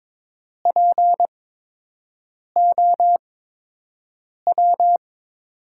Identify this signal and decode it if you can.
Morse code, heard as a keyed tone, about 22 wpm, spaced as for 6 wpm, 700 Hz. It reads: POW
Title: USA Call Signs - 5 Characters - 8x Character Spacing - 22wpm